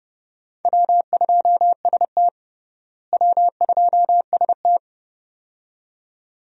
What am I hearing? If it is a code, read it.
Morse code, heard as a keyed tone, about 32 wpm, 700 Hz. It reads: W2ST W2ST
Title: USA Call Signs - 4 Characters - Repeated 2x - 30wpm